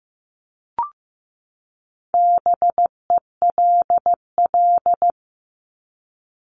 Morse code, heard as a keyed tone, about 15 wpm, 700 Hz. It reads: BELL
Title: Top 1751-2000 Words 15wpm